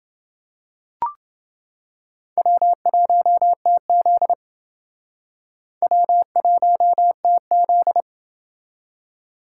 Morse code, heard as a keyed tone, about 29 wpm, 700 Hz. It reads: W1TZ W1TZ